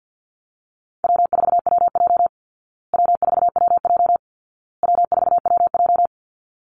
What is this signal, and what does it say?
Morse code, heard as a keyed tone, about 48 wpm, 700 Hz. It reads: W4WJ W4WJ W4WJ
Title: Call Signs Repeated 2x - 50wpm